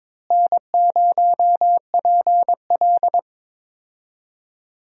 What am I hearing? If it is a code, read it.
Morse code, heard as a keyed tone, about 22 wpm, 700 Hz. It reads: N0PL
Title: USA Call Signs - 4 Characters - Repeated 2x - 22wpm